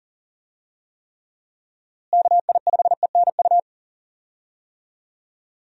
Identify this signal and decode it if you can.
Morse code, heard as a keyed tone, about 40 wpm, 700 Hz. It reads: KI5ENU
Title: USA Call Signs - 6 Characters - Repeated 2x - 40wpm